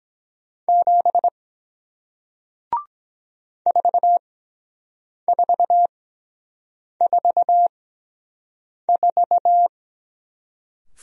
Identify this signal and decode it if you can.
Morse code, heard as a keyed tone, about 23 wpm, spaced as for 7 wpm, 700 Hz. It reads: 7 4444